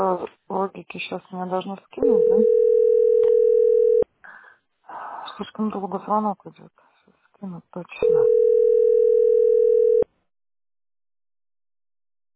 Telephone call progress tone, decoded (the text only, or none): ringback tone